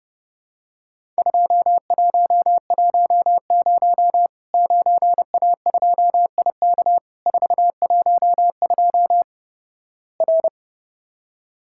Morse code, heard as a keyed tone, about 30 wpm, 700 Hz. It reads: 2110 9A2IK 412 R
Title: Worked All Europe DX Contest - QTC Exchange - 30wpm